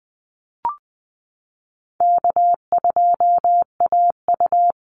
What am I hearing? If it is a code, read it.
Morse code, heard as a keyed tone, about 20 wpm, 700 Hz. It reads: K2AU
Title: Call Signs Repeated 3x - 20wpm